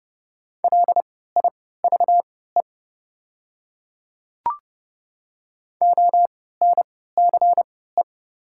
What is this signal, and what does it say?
Morse code, heard as a keyed tone, about 29 wpm, spaced as for 17 wpm, 690 Hz. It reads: LIVE ONCE
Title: 3x Character Spacing - Top 300 Words - 30wpm